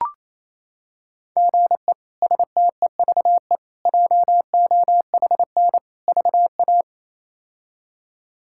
Morse code, heard as a keyed tone, about 27 wpm, 700 Hz. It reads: GE STEVE JOHN VA
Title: K1USN SST QSO Practice - 28wpm